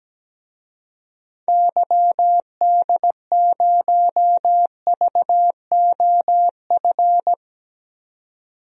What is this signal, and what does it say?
Morse code, heard as a keyed tone, about 17 wpm, 700 Hz. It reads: YD0VOF
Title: DXCC-100 Call Signs - 6 Characters - 17wpm